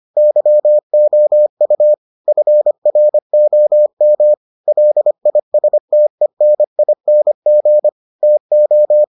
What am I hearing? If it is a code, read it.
Morse code, heard as a keyed tone, about 25 wpm, 600 Hz. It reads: YOU FROM LISTENING TO